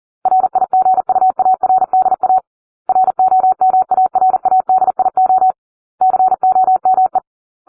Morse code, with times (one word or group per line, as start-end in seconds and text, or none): 0.3-2.4 RIGUARDA
2.9-5.5 ROMARADIO
6.0-7.2 COME